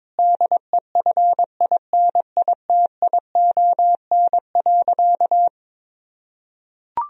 Morse code, heard as a keyed tone, about 22 wpm, 700 Hz. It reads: DEFINITION.
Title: Sentences from Top 1000 Words - Review Part 4 - 22wpm